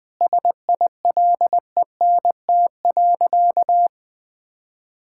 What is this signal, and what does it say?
Morse code, heard as a keyed tone, about 20 wpm, 700 Hz. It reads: SILENT.
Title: Sentences with 4-Word Phrases - 20wpm